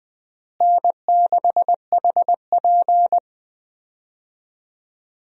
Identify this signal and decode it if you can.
Morse code, heard as a keyed tone, about 20 wpm, 700 Hz. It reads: N6HP
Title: USA Call Signs - 4 Characters - Repeated 2x - 20wpm